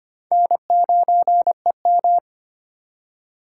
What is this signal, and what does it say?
Morse code, heard as a keyed tone, about 33 wpm, 700 Hz. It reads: N9EM